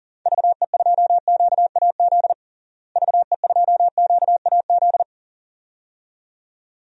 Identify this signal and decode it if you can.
Morse code, heard as a keyed tone, about 40 wpm, 700 Hz. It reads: VE2QAZ VE2QAZ
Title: DXCC-100 Call Signs - 6 Characters - Repeated 2x - 40wpm